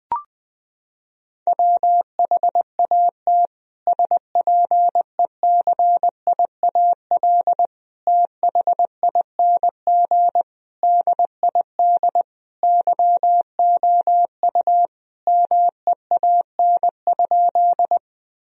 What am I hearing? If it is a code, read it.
Morse code, heard as a keyed tone, about 20 wpm, 700 Hz. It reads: WHAT SPECIAL THING DID YOU MEAN?
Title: Sentences from Top 1000 Words - Review Part 3 - 20wpm